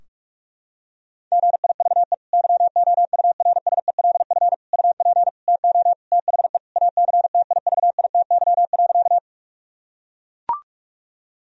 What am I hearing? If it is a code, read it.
Morse code, heard as a keyed tone, about 41 wpm, 700 Hz. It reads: GIVE YOURSELF UP TO THE ACTIVITY.
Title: Sentences from Top 1000 Words - Review Part 4 - 45wpm